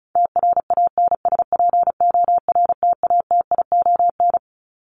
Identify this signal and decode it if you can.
Morse code, heard as a keyed tone, about 34 wpm, 700 Hz. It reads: TRANSPORTATION
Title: Top 1501-1750 Words 35wpm